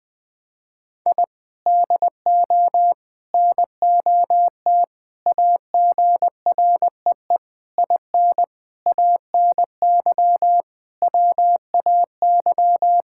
Morse code, heard as a keyed tone, about 20 wpm, 700 Hz. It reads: I DO NOT AGREE IN ANY WAY
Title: Sentences with 6-Word Phrases - 20wpm